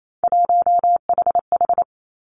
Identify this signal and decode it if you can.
Morse code, heard as a keyed tone, about 28 wpm, 700 Hz. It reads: 1HH
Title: Three Challenging Letter-Number Combinations 28wpm